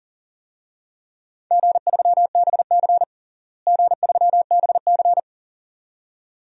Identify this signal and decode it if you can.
Morse code, heard as a keyed tone, about 40 wpm, 700 Hz. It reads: G3BC G3BC
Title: DXCC-100 Call Signs - 4 Characters - Repeated 2x - 40wpm